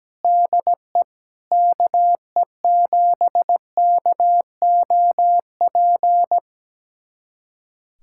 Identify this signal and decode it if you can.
Morse code, heard as a keyed tone, about 17 wpm, 700 Hz. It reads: DE KE7KOP
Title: USA Callsign DE Callsign - 6 Characters - 17wpm